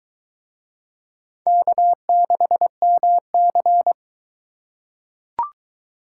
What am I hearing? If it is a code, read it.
Morse code, heard as a keyed tone, about 22 wpm, 700 Hz. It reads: K6MC